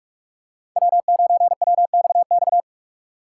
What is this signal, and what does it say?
Morse code, heard as a keyed tone, about 42 wpm, 700 Hz. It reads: W9WXX